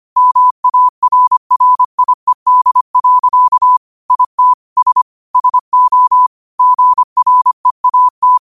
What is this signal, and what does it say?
Morse code, heard as a keyed tone, about 25 wpm, 1000 Hz. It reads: MARRIED. IT S SO GREAT